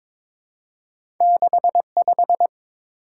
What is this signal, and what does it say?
Morse code, heard as a keyed tone, about 30 wpm, spaced as for 26 wpm, 700 Hz. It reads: 65
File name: Speed-Racing - Two Letter-Number Combinations - 22wpm